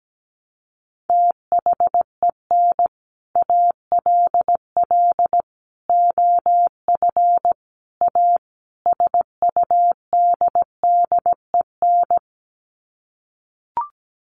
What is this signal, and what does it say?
Morse code, heard as a keyed tone, about 17 wpm, 700 Hz. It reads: THEN ALL OF A SUDDEN
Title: Sets of 5 Words Encore 17wpm